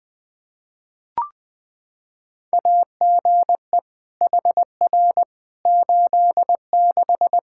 Morse code, heard as a keyed tone, about 20 wpm, 700 Hz. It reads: AGE HR 86
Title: Practice QSO - Exchange 3 - More Info - 20wpm